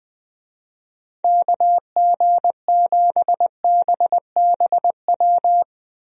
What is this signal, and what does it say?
Morse code, heard as a keyed tone, about 20 wpm, 700 Hz. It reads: KG7BBW
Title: USA Call Signs - 6 Characters - Repeated 3x - 20wpm